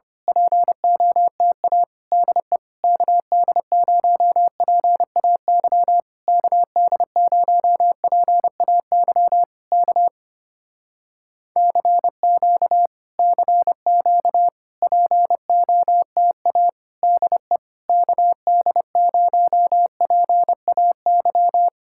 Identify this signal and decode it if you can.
Morse code, heard as a keyed tone, about 24 wpm, 700 Hz. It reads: POTA DE KD0PAY KD0PAY K CQ CQ POTA DE KD0PAY
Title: Speed-Racing Common QSO Phrases - 25wpm